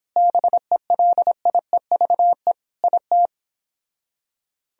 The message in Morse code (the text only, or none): BELIEVE IT